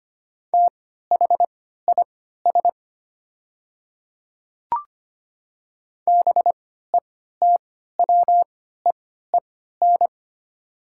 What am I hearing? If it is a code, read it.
Morse code, heard as a keyed tone, about 25 wpm, spaced as for 14 wpm, 700 Hz. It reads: THIS BETWEEN
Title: 3x Character Spacing - Top 300 Words - 25wpm